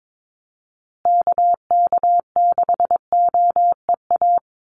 Morse code, heard as a keyed tone, about 22 wpm, 700 Hz. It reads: KK6OEA
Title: USA Call Signs Repeated 2x 22wpm